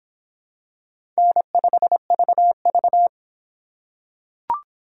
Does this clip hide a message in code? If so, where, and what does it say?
Yes: Morse code from 1.2 to 3.1 s, heard as a keyed tone, about 26 wpm, 700 Hz. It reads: N5VV